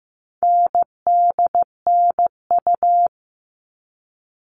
Morse code, heard as a keyed tone, about 15 wpm, 700 Hz. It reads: NDNU
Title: All Cut Numbers - 4-Digit 15wpm